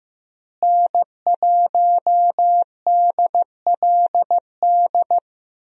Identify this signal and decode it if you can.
Morse code, heard as a keyed tone, about 15 wpm, 700 Hz. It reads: N1DLD
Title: USA Call Signs - 5 Characters - Repeated 2x - 15wpm